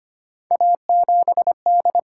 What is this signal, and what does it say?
Morse code, heard as a keyed tone, about 25 wpm, 700 Hz. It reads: A7D
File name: Three Letter-Number Combinations - 25wpm